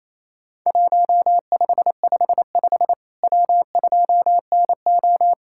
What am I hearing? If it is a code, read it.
Morse code, heard as a keyed tone, about 28 wpm, 700 Hz. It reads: 1555 W2NO
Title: Worked All Europe DX Contest - QTC Exchange - 28wpm